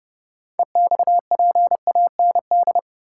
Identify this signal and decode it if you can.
Morse code, heard as a keyed tone, about 32 wpm, 700 Hz. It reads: EXPAND
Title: Sentences from Top 1400 Words - New Word Drill - 30wpm